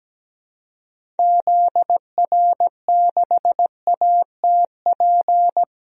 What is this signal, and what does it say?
Morse code, heard as a keyed tone, about 17 wpm, 700 Hz. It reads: ZR6ATP
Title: DXCC-100 Call Signs - 6 Characters - 17wpm